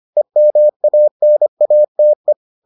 Morse code, heard as a keyed tone, about 25 wpm, 600 Hz. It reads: EMANATE